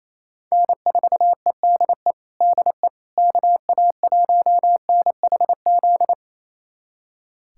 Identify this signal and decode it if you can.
Morse code, heard as a keyed tone, about 28 wpm, 700 Hz. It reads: N4EDE DE KA1NHZ
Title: USA Callsign DE Callsign 28wpm